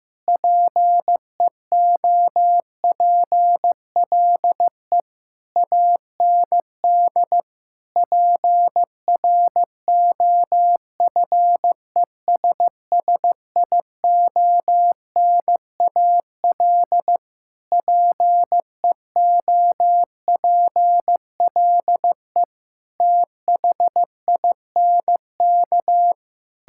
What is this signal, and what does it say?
Morse code, heard as a keyed tone, about 15 wpm, 700 Hz. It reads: PEOPLE AND PROFESSIONAL PEOPLE THINK
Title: Sentences from Top 1000 Words - Review Part 5 - 15wpm